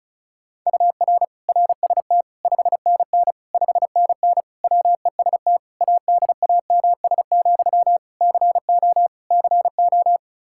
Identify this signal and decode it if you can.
Morse code, heard as a keyed tone, about 34 wpm, 700 Hz. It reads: UR RST 5NN 5NN WEST ADAMS, CO CO
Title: Practice QSO - Exchange 1 - Beginning - 35wpm